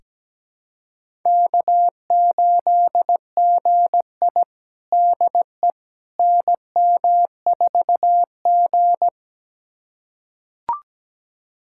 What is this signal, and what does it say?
Morse code, heard as a keyed tone, about 17 wpm, 700 Hz. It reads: K8GI DE NM4G
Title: USA Callsign DE Callsign - 4 Characters - 17wpm